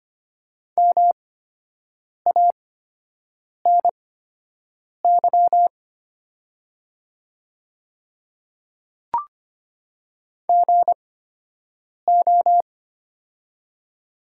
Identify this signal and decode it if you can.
Morse code, heard as a keyed tone, about 25 wpm, spaced as for 7 wpm, 700 Hz. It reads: MANY GO
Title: 8x Character Spacing - Top 100 Words - 25wpm